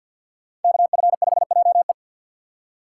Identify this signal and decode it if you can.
Morse code, heard as a keyed tone, about 46 wpm, 700 Hz. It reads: KF5JE
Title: USA Call Signs - 5 Characters - 4x Character Spacing - 50wpm